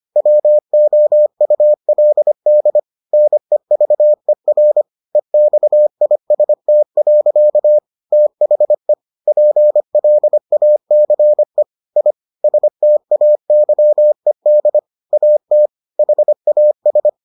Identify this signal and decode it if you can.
Morse code, heard as a keyed tone, about 25 wpm, 600 Hz. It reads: WOULD NEVER EXIST. THE PLACE I STAYED AT HAS